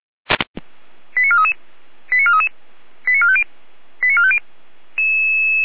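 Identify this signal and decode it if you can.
Five-tone selective call (ZVEI), heard as quick runs of five tones, tones of about 70 ms. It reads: 89433 89433 89450 89450
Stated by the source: ZVEI five-tone selective call received on SDR.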